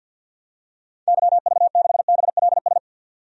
Morse code, heard as a keyed tone, about 52 wpm, 700 Hz. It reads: YV6BLS
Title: DXCC-100 Call Signs - 6 Characters - 50wpm